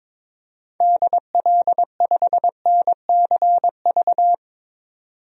Morse code, heard as a keyed tone, about 22 wpm, 700 Hz. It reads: DL5NCV
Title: DXCC-100 Call Signs - 6 Characters - 22wpm